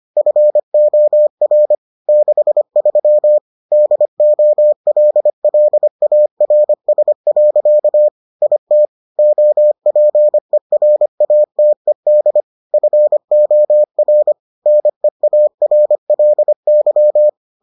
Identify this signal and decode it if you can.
Morse code, heard as a keyed tone, about 25 wpm, 600 Hz. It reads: FOR 63 DOLLARS. IT OPERATED FOR NEARLY